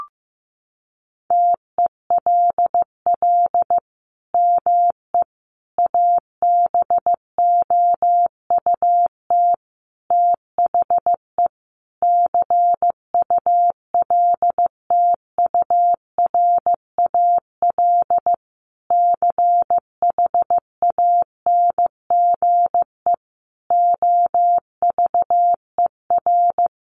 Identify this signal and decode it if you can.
Morse code, heard as a keyed tone, about 15 wpm, 700 Hz. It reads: TELL ME ABOUT THE CULTURAL CHANGE OVER
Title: Sentences from Top 1500 Words 15wpm